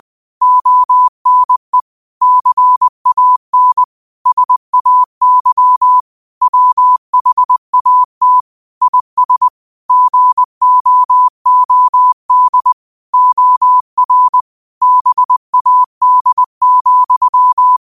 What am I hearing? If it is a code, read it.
Morse code, heard as a keyed tone, about 20 wpm, 1000 Hz. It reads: ONE CAN SAY WHAT IS GOOD OR BAD,